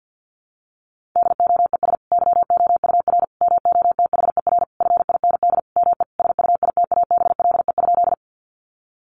Morse code, heard as a keyed tone, about 50 wpm, 700 Hz. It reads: DOES YOUR MOTHER FIND ME SUITABLE?
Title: Sentences from Top 1400 Words 50wpm